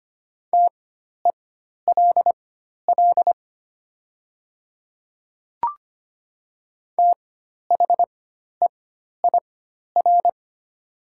Morse code, heard as a keyed tone, about 25 wpm, spaced as for 12 wpm, 690 Hz. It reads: TELL THEIR
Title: 4x Character Spacing - Top 200 Words - 25wpm